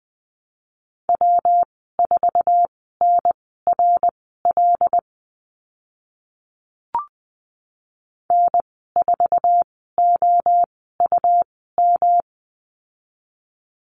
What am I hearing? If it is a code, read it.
Morse code, heard as a keyed tone, about 20 wpm, spaced as for 14 wpm, 700 Hz. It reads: W4NRL N4OUM
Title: 140 Lesson 23g - Letter f - Call Signs 20wpm